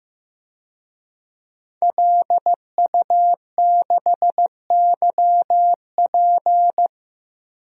Morse code, heard as a keyed tone, about 15 wpm, 700 Hz. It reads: LU6YP
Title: DXCC-100 Call Signs - 5 Characters - 15wpm